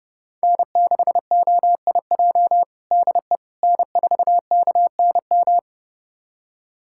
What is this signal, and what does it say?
Morse code, heard as a keyed tone, about 31 wpm, 700 Hz. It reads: N6OIJ DE N4KNM
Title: USA Callsign DE Callsign - 5 Characters - 30wpm